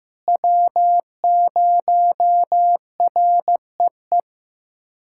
Morse code, heard as a keyed tone, about 15 wpm, 700 Hz. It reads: W0REE
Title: USA Callsign DE Callsign - 5 Characters - 15wpm